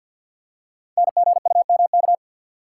Morse code, heard as a keyed tone, about 51 wpm, 700 Hz. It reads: NGUMK
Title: Code Groups - 5 Letters - 50wpm